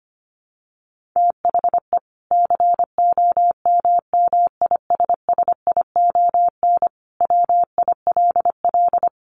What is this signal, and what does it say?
Morse code, heard as a keyed tone, about 25 wpm, 700 Hz. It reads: THE COMMISSION WILL
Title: Sentences from Top 1400 Words 25wpm